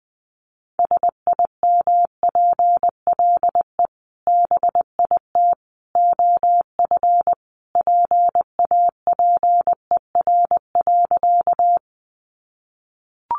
Morse code, heard as a keyed tone, about 20 wpm, 700 Hz. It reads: SIMPLE BIT OF PAPER.
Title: Sentences from Top 900 Words 20wpm